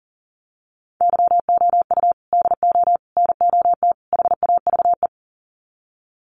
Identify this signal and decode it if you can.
Morse code, heard as a keyed tone, about 40 wpm, 700 Hz. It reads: YOU DO NOT HAVE
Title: Sets of 4 Words 40wpm